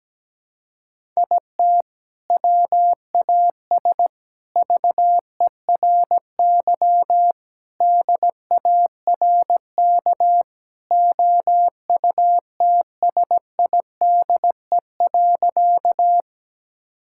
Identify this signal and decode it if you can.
Morse code, heard as a keyed tone, about 17 wpm, 700 Hz. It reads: IT WAS VERY DARK OUTSIDE.
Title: Sentences from Top 1000 Words - Review Part 6 - 17wpm